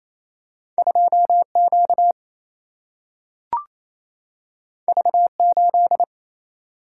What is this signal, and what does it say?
Morse code, heard as a keyed tone, about 28 wpm, 700 Hz. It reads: 2Q V8
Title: Single-Letter Single-Number 28wpm